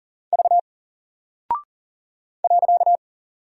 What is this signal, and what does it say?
Morse code, heard as a keyed tone, about 40 wpm, 700 Hz. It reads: V .
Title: Single Character (Letter:Number:Punctuation:Prosign) Alt 40wpm